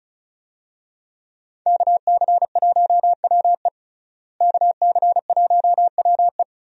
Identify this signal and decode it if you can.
Morse code, heard as a keyed tone, about 35 wpm, 700 Hz. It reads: KC1WE KC1WE
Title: USA Call Signs - 5 Characters - Repeated 2x - 35wpm